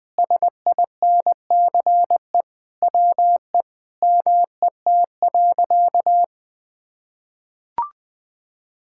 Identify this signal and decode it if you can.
Morse code, heard as a keyed tone, about 20 wpm, 700 Hz. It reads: SINCE WE MET.
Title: Sentences with 4-Word Phrases - Spoken Beginning - 20wpm